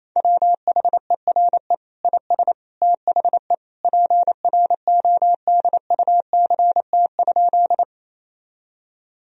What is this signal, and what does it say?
Morse code, heard as a keyed tone, about 28 wpm, 700 Hz. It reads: WHERE IS THE PRODUCT?
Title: Sentences from Top 1000 Words - Review Part 2 - 28wpm